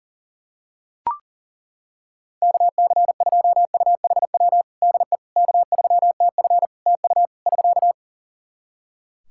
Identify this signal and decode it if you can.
Morse code, heard as a keyed tone, about 40 wpm, 700 Hz. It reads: KC2UHW DE K3TF TU <SK>
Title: Practice QSO - Exchange 4 - Ending - 40wpm